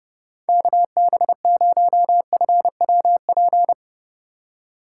Morse code, heard as a keyed tone, about 31 wpm, 700 Hz. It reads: KB0FWP
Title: USA Call Signs - 6 Characters - 2x Character Spacing - 30wpm